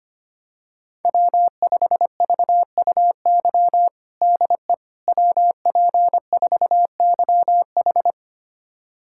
Morse code, heard as a keyed tone, about 25 wpm, 700 Hz. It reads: W5VUY DE WP4YH
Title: USA Callsign DE Callsign - 5 Characters - 25wpm